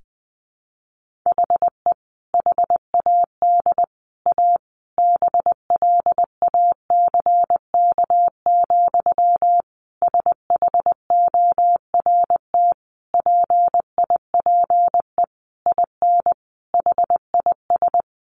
Morse code, heard as a keyed tone, about 20 wpm, 700 Hz. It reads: HE HAD A BLACK, SHORT PIPE IN HIS
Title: Sentences from Top 1500 Words 20wpm